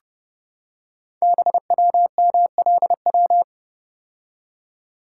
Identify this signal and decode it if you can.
Morse code, heard as a keyed tone, about 29 wpm, 700 Hz. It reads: BWMLW
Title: Code Groups - 5 Letters - 30wpm